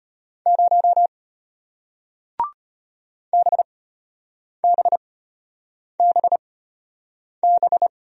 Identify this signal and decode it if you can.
Morse code, heard as a keyed tone, about 33 wpm, spaced as for 8 wpm, 700 Hz. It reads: 0 BBBB